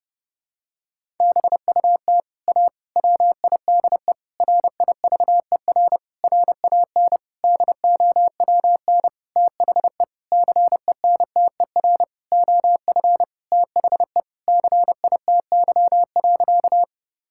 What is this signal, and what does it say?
Morse code, heard as a keyed tone, about 32 wpm, 700 Hz. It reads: BUT A WIDE RIVER RAN DOWN THE CENTER OF THE CITY.
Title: Sentences from Top 1000 Words - Review Part 6 - 30wpm